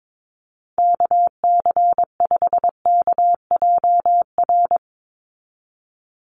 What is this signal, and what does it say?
Morse code, heard as a keyed tone, about 22 wpm, 700 Hz. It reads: KC5KJR
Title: USA Call Signs 22wpm